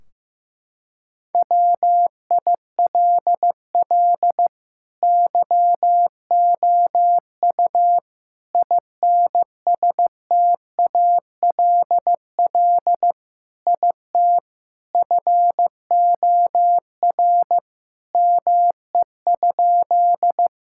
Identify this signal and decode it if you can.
Morse code, heard as a keyed tone, about 15 wpm, 700 Hz. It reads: WILL YOU INSTALL IT FOR ME?
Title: Sentences from Top 1600 Words 15wpm